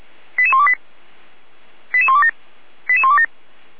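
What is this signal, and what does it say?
Five-tone selective call (ZVEI), heard as quick runs of five tones, tones of about 70 ms. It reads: 88128 88127 88127
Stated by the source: ZVEI five-tone call 88352 received by SDR.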